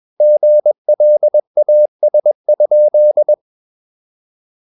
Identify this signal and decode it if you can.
Morse code, heard as a keyed tone, about 21 wpm, 600 Hz. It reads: GLAS?
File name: Ferie QRS 21wpm